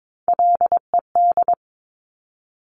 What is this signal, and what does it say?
Morse code, heard as a keyed tone, about 22 wpm, 700 Hz. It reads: LED
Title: Top Three Letter Combinations in Words 22wpm